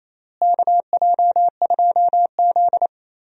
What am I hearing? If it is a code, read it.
Morse code, heard as a keyed tone, about 28 wpm, 700 Hz. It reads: KJ2Z
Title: USA Call Signs - 4 Characters - Repeated 2x - 28wpm